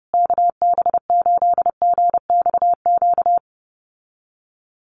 Morse code, heard as a keyed tone, about 30 wpm, 700 Hz. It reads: KB8GXQ
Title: USA Call Signs Repeated 2x 30wpm